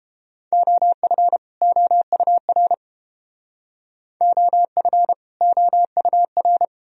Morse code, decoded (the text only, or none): OF OUR OF OUR